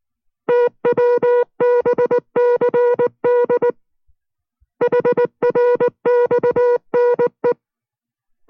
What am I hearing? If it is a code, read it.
Morse code, heard as a keyed tone, about 19 wpm, 490 Hz. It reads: TWBCD HRXNE